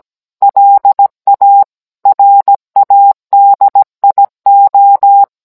Morse code, heard as a keyed tone, about 17 wpm, 800 Hz. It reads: LA RADIO